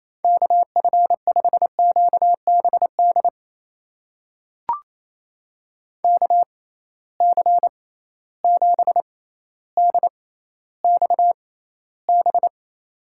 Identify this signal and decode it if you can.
Morse code, heard as a keyed tone, about 28 wpm, spaced as for 14 wpm, 700 Hz. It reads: KF5QBD KC7DXB
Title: USA Call Signs - 6 Characters - 6x Character Spacing - 28wpm